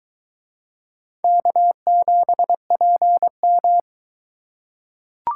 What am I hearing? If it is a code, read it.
Morse code, heard as a keyed tone, about 21 wpm, 700 Hz. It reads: K7PM